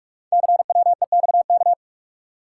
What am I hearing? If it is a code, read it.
Morse code, heard as a keyed tone, about 44 wpm, 690 Hz. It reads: CWEXK
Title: Code Groups - 5 Letters - 45wpm